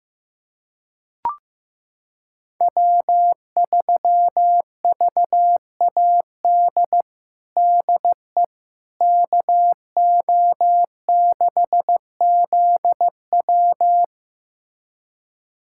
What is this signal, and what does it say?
Morse code, heard as a keyed tone, about 15 wpm, 700 Hz. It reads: W3VAD DE KO6ZW
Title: USA Callsign DE Callsign - 5 Characters - 15wpm